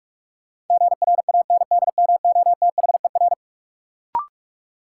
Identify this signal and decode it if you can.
Morse code, heard as a keyed tone, about 46 wpm, 700 Hz. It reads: GRANDMOTHER
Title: Sentences from Top 1200 Words - New Word Drill Alt - 45wpm